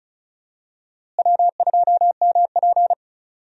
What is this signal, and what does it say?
Morse code, heard as a keyed tone, about 36 wpm, 700 Hz. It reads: W2MP
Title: USA Call Signs - 4 Characters - Repeated 3x - 35wpm